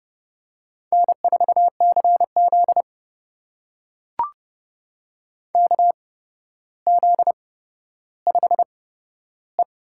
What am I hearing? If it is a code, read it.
Morse code, heard as a keyed tone, about 30 wpm, spaced as for 13 wpm, 700 Hz. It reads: N4CZ KZ5E
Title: USA Call Signs - 8x Character Spacing - 30wpm